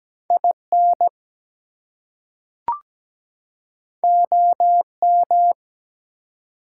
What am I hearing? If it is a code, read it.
Morse code, heard as a keyed tone, about 17 wpm, 700 Hz. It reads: IN OM
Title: Top 180 QSO Elements 17wpm